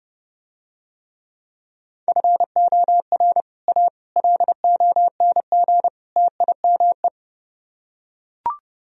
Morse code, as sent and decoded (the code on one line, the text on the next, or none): ..-. --- .-. / .- / .-.. --- -. --. / - .. -- .
FOR A LONG TIME